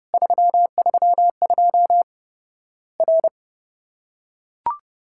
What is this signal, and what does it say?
Morse code, heard as a keyed tone, about 32 wpm, 700 Hz. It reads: 332 R
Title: Worked All Europe DX Contest - QTC Exchange - 30wpm